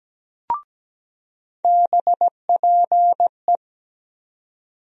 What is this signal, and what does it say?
Morse code, heard as a keyed tone, about 17 wpm, 700 Hz. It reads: BPE
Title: USA Call Sign Postfixes - 3 Letters 17wpm